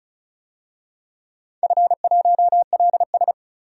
Morse code, heard as a keyed tone, about 36 wpm, 690 Hz. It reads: F1LS
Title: DXCC-100 Call Signs - 4 Characters - 35wpm